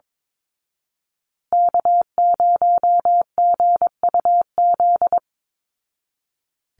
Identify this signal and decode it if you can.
Morse code, heard as a keyed tone, about 22 wpm, 700 Hz. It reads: K0GUZ
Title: Call Signs Repeated 3x - 22wpm